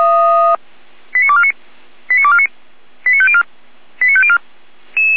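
Five-tone selective call (ZVEI), heard as quick runs of five tones, tones of about 70 ms. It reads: 89249 89249 89504 89504
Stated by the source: ZVEI five-tone selective call received on an SDR.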